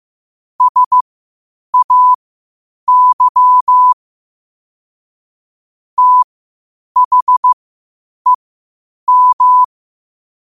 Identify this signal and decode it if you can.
Morse code, heard as a keyed tone, about 15 wpm, spaced as for 8 wpm, 1000 Hz. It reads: SAY THEM
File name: qotdfiles / 8wpm